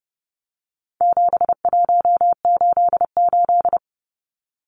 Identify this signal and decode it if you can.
Morse code, heard as a keyed tone, about 30 wpm, 700 Hz. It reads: 7188
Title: Numbers - 4-Digit 30wpm